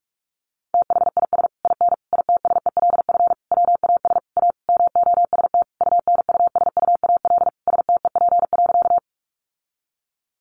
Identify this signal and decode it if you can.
Morse code, heard as a keyed tone, about 44 wpm, 700 Hz. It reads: THIS IN ITSELF WAS A MOST UNUSUAL STEP.
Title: Sentences from Top 1400 Words 45wpm